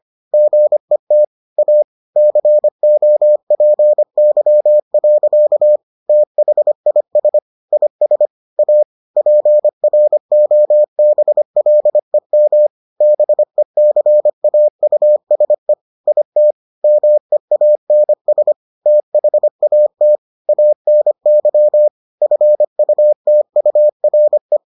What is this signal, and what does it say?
Morse code, heard as a keyed tone, about 25 wpm, 600 Hz. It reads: GET A COPY. THIS IS A PROBLEM BECAUSE IT MEANS THAT ANY FUTURE